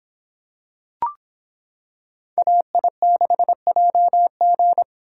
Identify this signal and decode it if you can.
Morse code, heard as a keyed tone, about 25 wpm, 700 Hz. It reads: AI6JG